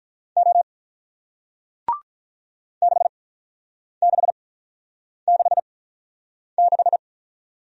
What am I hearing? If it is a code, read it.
Morse code, heard as a keyed tone, about 48 wpm, spaced as for 9 wpm, 700 Hz. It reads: O 6666